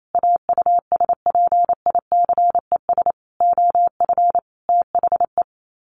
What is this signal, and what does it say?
Morse code, heard as a keyed tone, about 28 wpm, 700 Hz. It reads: AUSPICES OF THE
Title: Sets of 5 Words Encore2 28wpm